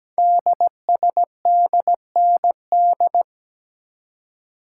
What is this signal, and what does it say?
Morse code, heard as a keyed tone, about 17 wpm, 700 Hz. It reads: DSDND
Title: Code Groups - 5 Letters - 17wpm